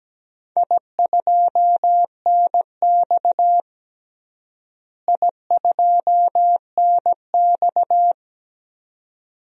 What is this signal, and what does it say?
Morse code, heard as a keyed tone, about 17 wpm, 700 Hz. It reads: I2NX I2NX
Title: DXCC-100 Call Signs - 4 Characters - Repeated 2x - 17wpm